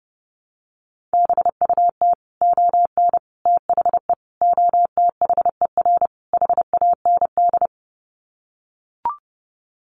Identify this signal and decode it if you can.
Morse code, heard as a keyed tone, about 28 wpm, 700 Hz. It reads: BUT ON THE OTHER HAND
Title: Sets of 5 Words 30wpm